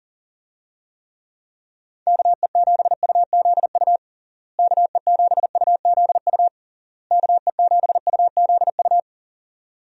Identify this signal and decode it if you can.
Morse code, heard as a keyed tone, about 40 wpm, 700 Hz. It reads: KE7UZU KE7UZU KE7UZU
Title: USA Call Signs - 6 Characters - Repeated 3x - 40wpm